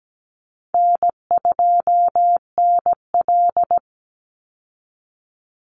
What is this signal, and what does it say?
Morse code, heard as a keyed tone, about 17 wpm, 700 Hz. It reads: N2NL
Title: Call Signs Repeated 2x - 17wpm